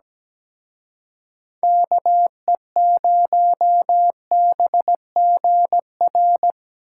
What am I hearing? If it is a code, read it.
Morse code, heard as a keyed tone, about 20 wpm, 700 Hz. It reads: KE0BGR
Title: Speed Racing - USA 2x3 Call Signs 17wpm